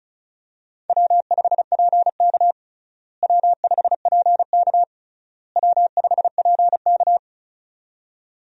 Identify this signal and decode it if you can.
Morse code, heard as a keyed tone, about 34 wpm, 700 Hz. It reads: W5PK W5PK W5PK
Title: USA Call Signs - 4 Characters - Repeated 3x - 35wpm